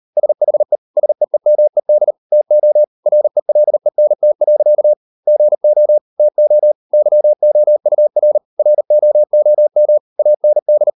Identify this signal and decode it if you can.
Morse code, heard as a keyed tone, about 38 wpm, 600 Hz. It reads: SHE SEEMED TO RELENT. GO TO YOUR ROOM AND